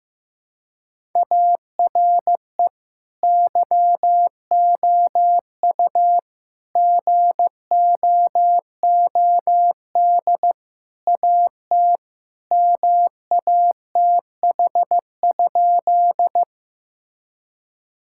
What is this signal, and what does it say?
Morse code, heard as a keyed tone, about 16 wpm, 700 Hz. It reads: ARE YOU GOOD AT MATH?
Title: Sentences from Top 1000 Words - Review Part 6 - 15wpm